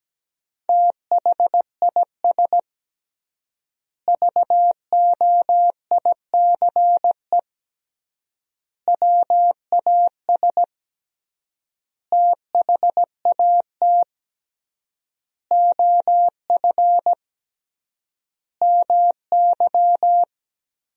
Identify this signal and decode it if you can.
Morse code, heard as a keyed tone, about 17 wpm, 700 Hz. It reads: THIS VOICE WAS THAT OF MY
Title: 3x Word-Spacing - Sentences from Top 500 Words - 17wpm